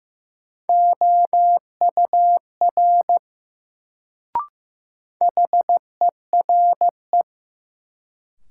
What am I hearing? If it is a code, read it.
Morse code, heard as a keyed tone, about 15 wpm, 700 Hz. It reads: OUR HERE
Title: Sentences from Top 200 Words - New Word Drill - 15wpm